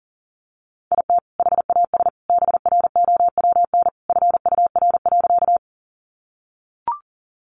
Morse code, heard as a keyed tone, about 40 wpm, 700 Hz. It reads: IT HAS BROWN FUR.
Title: Easy Phrases 40wpm